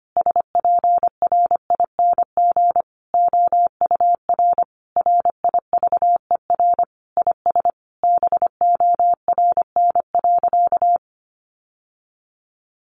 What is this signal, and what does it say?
Morse code, heard as a keyed tone, about 25 wpm, 700 Hz. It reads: SPRING OUR RIVER IS BORN.
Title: Sentences from Top 1200 Words 25wpm